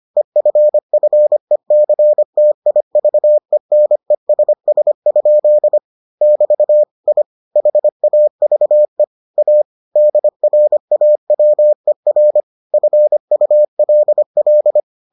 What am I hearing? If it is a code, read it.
Morse code, heard as a keyed tone, about 25 wpm, 600 Hz. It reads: EFFECTIVENESS? <BT> I HAVE A DRAWER FULL